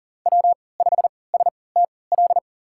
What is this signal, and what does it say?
Morse code, heard as a keyed tone, about 40 wpm, spaced as for 23 wpm, 700 Hz. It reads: W5STL
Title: USA Call Signs - 5 Characters - 3x Character Spacing - 40wpm